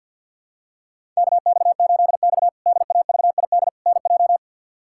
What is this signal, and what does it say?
Morse code, heard as a keyed tone, about 53 wpm, 700 Hz. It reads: KX8X DAVID NJ
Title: K1USN SST QSO Practice - 50wpm